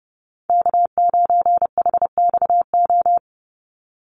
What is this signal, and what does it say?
Morse code, heard as a keyed tone, about 30 wpm, 700 Hz. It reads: K9HXO
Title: Call Signs Repeated 2x - 30wpm